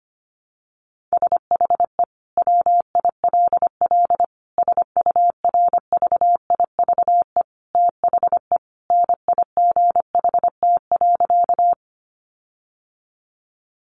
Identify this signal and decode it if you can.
Morse code, heard as a keyed tone, about 25 wpm, 700 Hz. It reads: SHE WILL SURVIVE THE NIGHT.
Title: Sentences from Top 1300 Words 25wpm